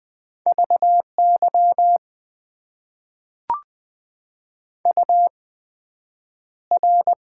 Morse code, heard as a keyed tone, about 20 wpm, spaced as for 9 wpm, 700 Hz. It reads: VY UR
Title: Top 90 QSO Elements - 8x Character Spacing - 20wpm